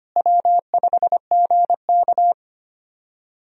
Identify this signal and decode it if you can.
Morse code, heard as a keyed tone, about 25 wpm, 700 Hz. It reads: W5GK
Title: USA Call Signs - 4 Characters - Repeated 3x - 25wpm